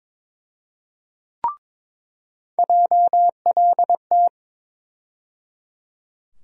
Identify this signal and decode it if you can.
Morse code, heard as a keyed tone, about 22 wpm, 700 Hz. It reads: JLT
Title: USA Call Sign Postfixes - 3 Letters 22wpm